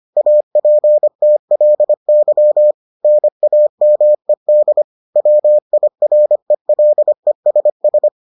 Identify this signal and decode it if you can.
Morse code, heard as a keyed tone, about 25 wpm, 600 Hz. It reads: APTLY NAMED WIRELESS